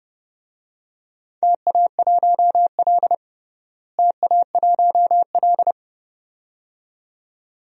Morse code, heard as a keyed tone, about 30 wpm, 700 Hz. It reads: TA1L TA1L
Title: DXCC-100 Call Signs - 4 Characters - Repeated 2x - 30wpm